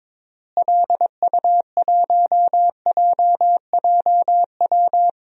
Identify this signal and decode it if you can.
Morse code, heard as a keyed tone, about 21 wpm, 700 Hz. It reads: LU1JJW
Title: DXCC-100 Call Signs - 6 Characters - Repeated 3x - 22wpm